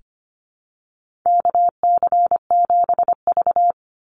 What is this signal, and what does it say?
Morse code, heard as a keyed tone, about 25 wpm, 700 Hz. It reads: KC7V
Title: Call Signs - 25wpm